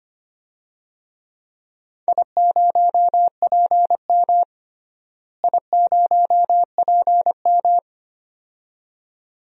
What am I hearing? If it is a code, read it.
Morse code, heard as a keyed tone, about 25 wpm, 700 Hz. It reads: I0PM I0PM
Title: DXCC-100 Call Signs - 4 Characters - Repeated 2x - 25wpm